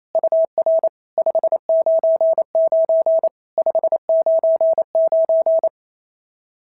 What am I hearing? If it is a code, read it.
Morse code, heard as a keyed tone, about 28 wpm, 650 Hz. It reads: UR 599 599
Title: SOTA QSO Practice 28wpm